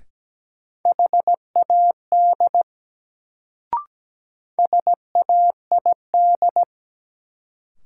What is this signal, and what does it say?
Morse code, heard as a keyed tone, about 17 wpm, 700 Hz. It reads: HAD SAID
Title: Sentences from Top 100 Words - New Word Drill - 17wpm